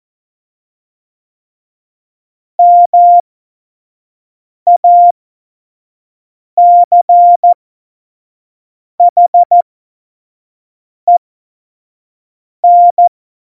Morse code, heard as a keyed tone, about 14 wpm, spaced as for 5 wpm, 700 Hz. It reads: MACHEN